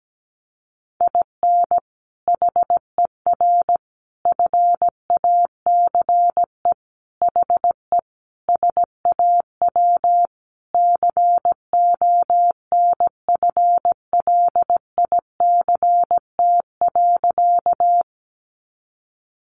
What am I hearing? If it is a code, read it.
Morse code, heard as a keyed tone, about 17 wpm, 700 Hz. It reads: IN HER FACE HE SAW CONFLICT.
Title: Sentences from Top 1300 Words 17wpm